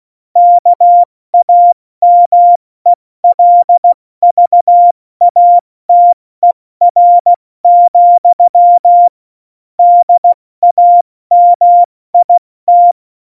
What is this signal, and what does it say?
Morse code, heard as a keyed tone, about 16 wpm, 700 Hz. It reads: KAMELVATER, DAMIT
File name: Morsekurs / Witze / Witz 06. November 2018, 70 F80